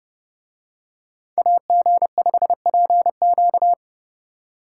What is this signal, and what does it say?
Morse code, heard as a keyed tone, about 30 wpm, 700 Hz. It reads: AG5PQ